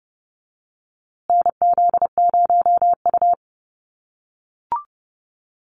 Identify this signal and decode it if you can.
Morse code, heard as a keyed tone, about 29 wpm, 700 Hz. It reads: NZ0U
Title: USA Call Signs Repeated 3x 30wpm